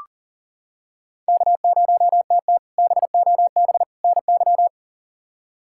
Morse code, heard as a keyed tone, about 37 wpm, 700 Hz. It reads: K0TT BOB NY
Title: K1USN SST QSO Practice - 40wpm